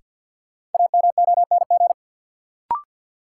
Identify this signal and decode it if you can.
Morse code, heard as a keyed tone, about 53 wpm, 700 Hz. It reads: AMONG
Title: Sentences from Top 500 Words - New Word Drill - 50wpm